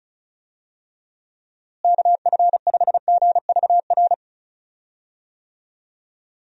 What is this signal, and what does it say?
Morse code, heard as a keyed tone, about 34 wpm, 700 Hz. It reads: KF5GVR
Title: USA Call Signs - 6 Characters - Repeated 3x - 35wpm